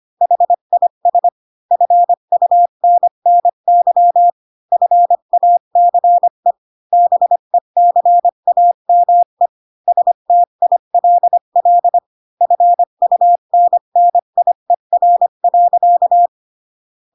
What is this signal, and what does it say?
Morse code, heard as a keyed tone, about 25 wpm, 700 Hz. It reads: HIS FUNNY FACE BECAME STILL FUNNIER.
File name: Pinocchio / Pinocchio ch1 20 25 ik1hge